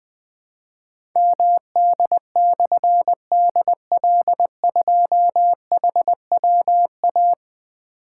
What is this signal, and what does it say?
Morse code, heard as a keyed tone, about 20 wpm, 700 Hz. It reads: MD/DL2HWA
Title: Difficult Call Signs 20wpm